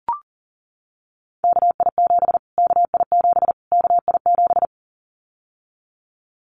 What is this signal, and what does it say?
Morse code, heard as a keyed tone, about 40 wpm, 700 Hz. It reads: KI7 KI7 KI7
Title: Callsign Prefixes for Top 25 DXCC - Repeated 3x - 40wpm